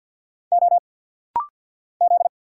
Morse code, heard as a keyed tone, about 50 wpm, 690 Hz. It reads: O Z
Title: Single Letters - Rapid-Fire 50wpm